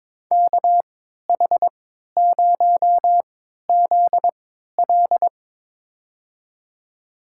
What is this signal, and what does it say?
Morse code, heard as a keyed tone, about 23 wpm, spaced as for 13 wpm, 700 Hz. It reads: KH0ZL
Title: USA Call Signs - 5 Characters - 3x Character Spacing - 22wpm